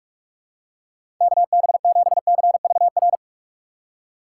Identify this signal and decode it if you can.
Morse code, heard as a keyed tone, about 44 wpm, 700 Hz. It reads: KB7CVR
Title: USA Call Signs - 6 Characters - 6x Character Spacing - 45wpm